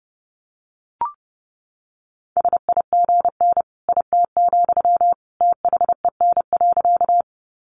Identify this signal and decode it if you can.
Morse code, heard as a keyed tone, about 30 wpm, 700 Hz. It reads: SIGN IT, THEN.
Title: Sentences from Top 1000 Words 30wpm